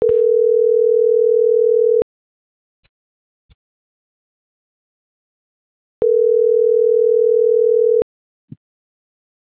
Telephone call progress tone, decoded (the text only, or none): ringback tone